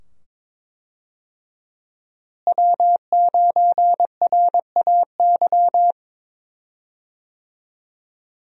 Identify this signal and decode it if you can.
Morse code, heard as a keyed tone, about 22 wpm, 700 Hz. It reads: W9RAY
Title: USA Call Signs - 5 Characters - Repeated 2x - 22wpm